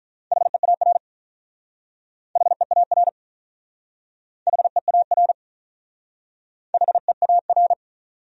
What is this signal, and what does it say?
Morse code, heard as a keyed tone, about 44 wpm, 700 Hz. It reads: HEAR HEAR HEAR HEAR